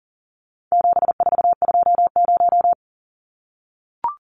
Morse code, heard as a keyed tone, about 40 wpm, 700 Hz. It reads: 7420
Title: Numbers - 4-Digit 40wpm